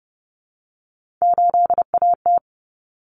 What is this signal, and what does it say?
Morse code, heard as a keyed tone, about 30 wpm, 700 Hz. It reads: 8AT
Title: Common Cut Numbers - 3-Digit 30wpm